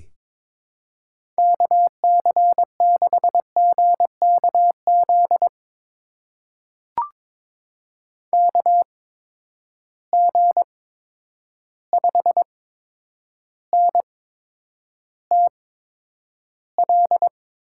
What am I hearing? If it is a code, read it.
Morse code, heard as a keyed tone, about 22 wpm, spaced as for 9 wpm, 700 Hz. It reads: KC6GKZ KG5NTL
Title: USA Call Signs - 6 Characters - 8x Character Spacing - 22wpm